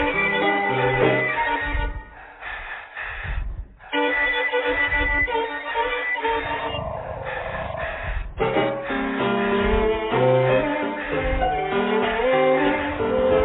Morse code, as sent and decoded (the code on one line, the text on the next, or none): none